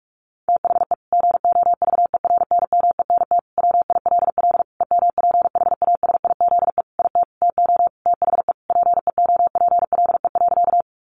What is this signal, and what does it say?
Morse code, heard as a keyed tone, about 45 wpm, 700 Hz. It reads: THE GOVERNMENT WILL EMPHASIZE IT TO THE PEOPLE.
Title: Sentences from Top 1500 Words 45wpm